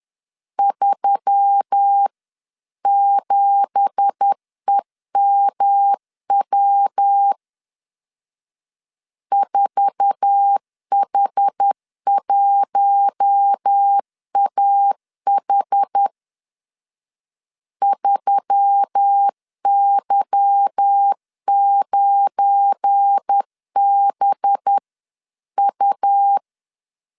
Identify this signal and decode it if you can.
Morse code, heard as a keyed tone, about 10 wpm, 800 Hz. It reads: 37EMW 4H1AH 3Y9BU